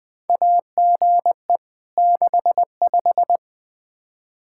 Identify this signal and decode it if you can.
Morse code, heard as a keyed tone, about 20 wpm, 700 Hz. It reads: AGE 65
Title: K1USN SST QSO Practice 20wpm